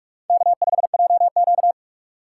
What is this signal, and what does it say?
Morse code, heard as a keyed tone, about 44 wpm, 700 Hz. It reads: K5JQ
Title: USA Call Signs - 4 Characters - 4x Character Spacing - 45wpm